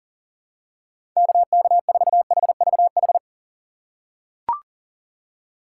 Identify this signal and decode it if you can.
Morse code, heard as a keyed tone, about 40 wpm, 700 Hz. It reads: KK4HVH
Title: USA Call Signs - 6x Character Spacing - 40wpm